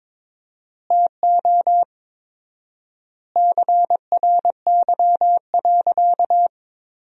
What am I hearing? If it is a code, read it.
Morse code, heard as a keyed tone, about 21 wpm, 700 Hz. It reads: TO CRY.
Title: 4x Word-Spacing - Sentences from Top 500 Words - 22wpm